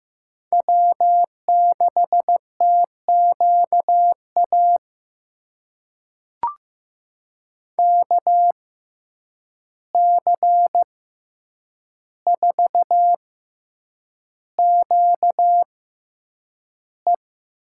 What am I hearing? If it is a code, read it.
Morse code, heard as a keyed tone, about 15 wpm, spaced as for 8 wpm, 700 Hz. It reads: W6TQA KC4QE
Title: USA Call Signs - 5 Characters - 6x Character Spacing - 15wpm